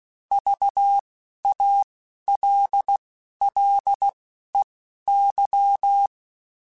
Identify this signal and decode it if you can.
Morse code, heard as a keyed tone, about 16 wpm, spaced as for 11 wpm, 800 Hz. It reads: VALLEY